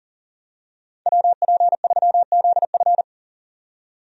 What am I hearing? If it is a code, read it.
Morse code, heard as a keyed tone, about 40 wpm, 700 Hz. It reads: WP3ZF
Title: USA Call Signs - 5 Characters - 40wpm